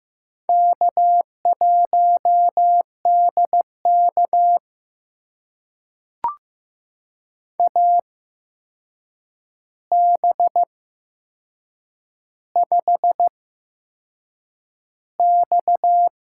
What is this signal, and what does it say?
Morse code, heard as a keyed tone, about 15 wpm, spaced as for 6 wpm, 700 Hz. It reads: K1DK AB5X
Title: USA Call Signs - 4 Characters - 8x Character Spacing - 15wpm